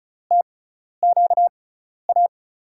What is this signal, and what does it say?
Morse code, heard as a keyed tone, about 36 wpm, spaced as for 12 wpm, 700 Hz. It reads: TQA
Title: USA Call Signs - 5 Characters - 6x Character Spacing - 35wpm